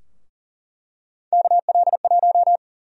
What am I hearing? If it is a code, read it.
Morse code, heard as a keyed tone, about 37 wpm, 700 Hz. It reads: KL1
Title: USA Call Sign Prefixes - Repeated 2x - 40wpm